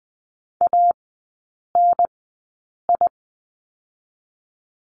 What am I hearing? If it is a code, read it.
Morse code, heard as a keyed tone, about 20 wpm, spaced as for 8 wpm, 700 Hz. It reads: ANI
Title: 053 Lesson 10e - Letter r - Three Characters 20wpm